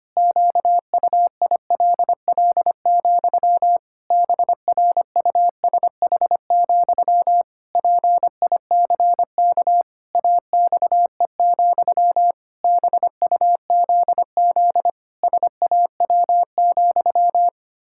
Morse code, heard as a keyed tone, about 25 wpm, 700 Hz. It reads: QUILL, BRUSH, PICK AXE, BUZZ SAW,
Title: Pangram Challenge v2 - 25wpm